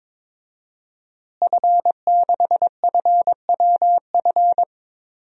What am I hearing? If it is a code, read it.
Morse code, heard as a keyed tone, about 22 wpm, 700 Hz. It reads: F6FWF
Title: DXCC-100 Call Signs - 5 Characters - Repeated 2x - 22wpm